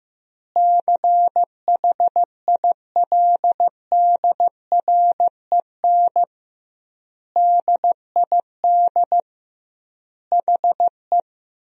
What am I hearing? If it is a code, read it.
Morse code, heard as a keyed tone, about 16 wpm, 700 Hz. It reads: CHILDREN DID HE
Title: 2x Word-Spacing - Sentences from Top 400 Words - 15wpm